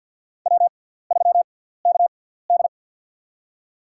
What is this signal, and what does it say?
Morse code, heard as a keyed tone, about 47 wpm, spaced as for 17 wpm, 690 Hz. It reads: W3KD
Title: USA Call Signs - 4 Characters - 6x Character Spacing - 50wpm